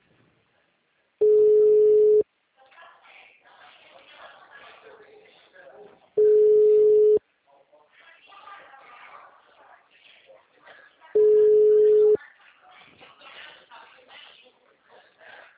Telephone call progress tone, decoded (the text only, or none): ringback tone